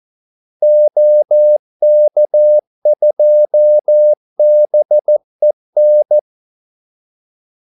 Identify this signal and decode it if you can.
Morse code, heard as a keyed tone, about 14 wpm, 600 Hz. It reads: OK2BEN
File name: Callsigns-14wpm